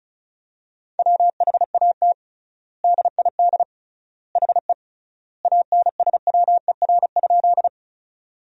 Morse code, heard as a keyed tone, about 36 wpm, 700 Hz. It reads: WHAT DID HE ANSWER?
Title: 3x Word-Spacing - Sentences from Top 300 Words - 35wpm